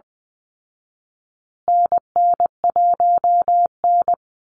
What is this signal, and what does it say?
Morse code, heard as a keyed tone, about 20 wpm, 700 Hz. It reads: NN1N
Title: Call Signs Repeated 3x - 20wpm